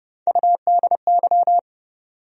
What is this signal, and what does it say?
Morse code, heard as a keyed tone, about 30 wpm, 700 Hz. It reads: UDY
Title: USA Call Sign Postfixes - 3 Letters 30wpm